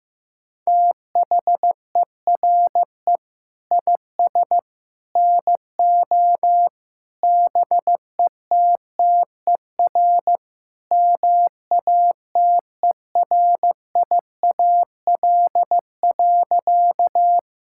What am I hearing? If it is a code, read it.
Morse code, heard as a keyed tone, about 15 wpm, 690 Hz. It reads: THERE IS NO BETTER MATERIAL.
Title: Sentences from Top 1000 Words - Review Part 5 - 15wpm